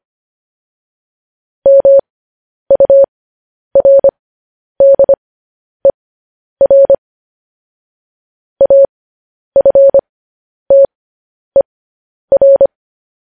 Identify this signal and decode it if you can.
Morse code, heard as a keyed tone, about 25 wpm, spaced as for 10 wpm, 550 Hz. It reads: MURDER AFTER
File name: podcast / News headlines-1x-25x10